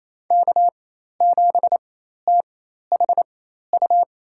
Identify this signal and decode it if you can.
Morse code, heard as a keyed tone, about 28 wpm, spaced as for 13 wpm, 700 Hz. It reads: K7THU
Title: USA Call Signs - 5 Characters - 4x Character Spacing - 28wpm